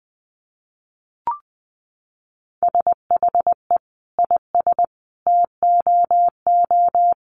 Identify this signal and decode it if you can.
Morse code, heard as a keyed tone, about 20 wpm, 700 Hz. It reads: SHE IS TOO
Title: Sentences from Top 1300 Words 20wpm